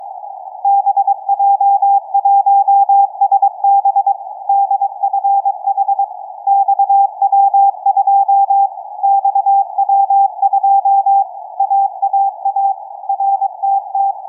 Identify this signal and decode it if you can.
Morse code, heard as a keyed tone, about 23 wpm, 790 Hz. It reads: BJ1SB DFH XW2 XW2 AAA RTT
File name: XW2A-CAS3A-VERT-01 17h46 utc 145mhz660 cw